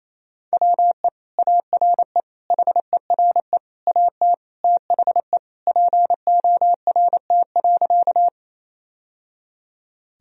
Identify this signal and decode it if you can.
Morse code, heard as a keyed tone, about 28 wpm, 700 Hz. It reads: WE ARE HERE AT THE PORT.
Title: Sentences from Top 1000 Words - Review Part 1 - 28wpm